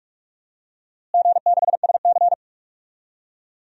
Morse code, heard as a keyed tone, about 45 wpm, 700 Hz. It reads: G6SC
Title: DXCC-100 Call Signs - 4 Characters - Repeated 3x - 45wpm